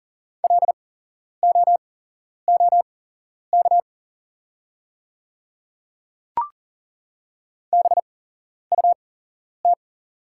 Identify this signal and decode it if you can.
Morse code, heard as a keyed tone, about 40 wpm, spaced as for 11 wpm, 690 Hz. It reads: LOOK BUT
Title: 8x Character Spacing - Top 200 Words - 40wpm